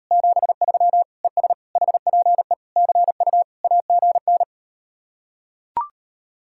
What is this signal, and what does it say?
Morse code, heard as a keyed tone, about 38 wpm, 700 Hz. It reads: 73 ES HPE CU AGN